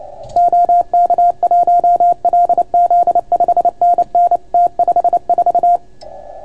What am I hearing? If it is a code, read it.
Morse code, heard as a keyed tone, about 29 wpm, 690 Hz. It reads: OK1LZ 5NN T54